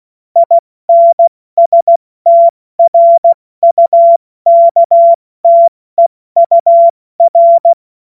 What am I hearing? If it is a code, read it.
Morse code, heard as a keyed tone, about 16 wpm, 680 Hz. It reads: INSTRUKTEUR
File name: Morsekurs / Witze / Witz 07. Dezember 2021, 70 F80